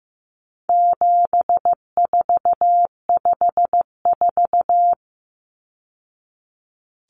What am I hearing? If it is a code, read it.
Morse code, heard as a keyed tone, about 15 wpm, 700 Hz. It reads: 7454
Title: Numbers - 4-Digit 15wpm